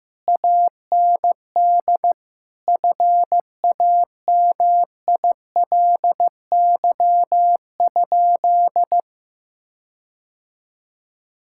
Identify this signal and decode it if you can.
Morse code, heard as a keyed tone, about 16 wpm, 700 Hz. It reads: AND FAMILY?
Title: Sentences from Top 1600 Words 15wpm